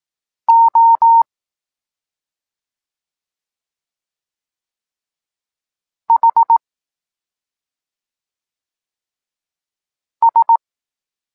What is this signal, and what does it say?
Morse code, heard as a keyed tone, about 18 wpm, 920 Hz. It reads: O H S
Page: Tekst 14 – doskonalenie odbioru poznanych liter